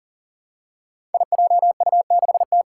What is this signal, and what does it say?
Morse code, heard as a keyed tone, about 37 wpm, 700 Hz. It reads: IJU6T
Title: Five Challenging Letter-Number Combinations 40wpm